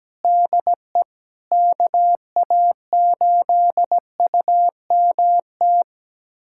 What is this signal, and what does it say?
Morse code, heard as a keyed tone, about 17 wpm, 700 Hz. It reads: DE KA8UMT
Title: USA Callsign DE Callsign - 6 Characters - 17wpm